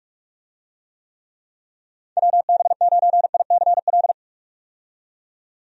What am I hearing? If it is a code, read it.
Morse code, heard as a keyed tone, about 46 wpm, 700 Hz. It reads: WB9ICL
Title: USA Call Signs - 6 Characters - 45wpm